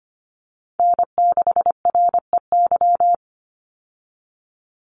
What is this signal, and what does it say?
Morse code, heard as a keyed tone, about 25 wpm, 700 Hz. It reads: N6REY
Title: USA Call Signs 25wpm